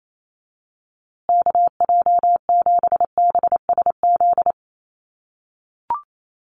Morse code, heard as a keyed tone, about 28 wpm, 700 Hz. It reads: KJ7BSZ
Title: USA Call Signs Repeated 2x 28wpm